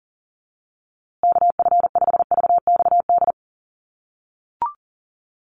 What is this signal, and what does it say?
Morse code, heard as a keyed tone, about 40 wpm, 700 Hz. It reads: KF5VXD
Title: USA Call Signs Repeated 3x 40wpm